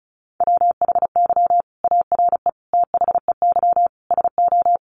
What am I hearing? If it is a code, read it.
Morse code, heard as a keyed tone, about 34 wpm, 700 Hz. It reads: WHY ARE THEY SO
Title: Sentences from Top 1200 Words 35wpm